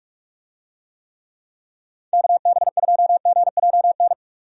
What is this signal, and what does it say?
Morse code, heard as a keyed tone, about 46 wpm, 700 Hz. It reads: KB2GJN
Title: USA Call Signs - 6 Characters - 45wpm